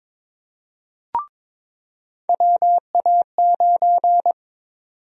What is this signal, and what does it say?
Morse code, heard as a keyed tone, about 21 wpm, 700 Hz. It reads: WA9
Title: USA Call Sign Prefixes - 22wpm